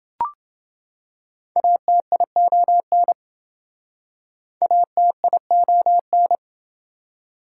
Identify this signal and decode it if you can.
Morse code, heard as a keyed tone, about 27 wpm, 700 Hz. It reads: ATION ATION